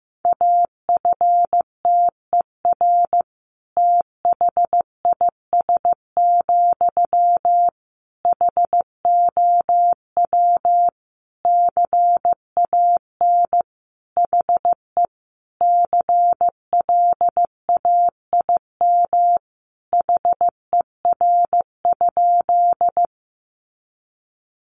Morse code, heard as a keyed tone, about 15 wpm, 700 Hz. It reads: AFTER THIS, HOW CAN HE CLAIM HER?
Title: Sentences from Top 1500 Words 15wpm